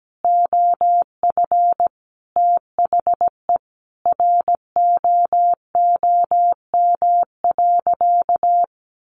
Morse code, heard as a keyed tone, about 17 wpm, 700 Hz. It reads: OF THE ROOM.
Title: Sentences from Top 1500 Words 17wpm